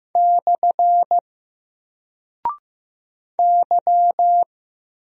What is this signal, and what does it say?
Morse code, heard as a keyed tone, about 15 wpm, 700 Hz. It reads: / Y
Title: Single Character (Letter:Number:Punctuation:Prosign) 15wpm